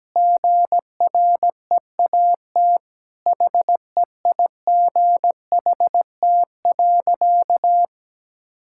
Morse code, heard as a keyed tone, about 17 wpm, 700 Hz. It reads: GREAT HEIGHT.
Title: Sentences from Top 1600 Words 17wpm